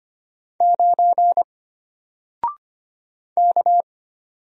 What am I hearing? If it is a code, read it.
Morse code, heard as a keyed tone, about 25 wpm, 700 Hz. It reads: 9 K
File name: Single-Letter-Number-25wpm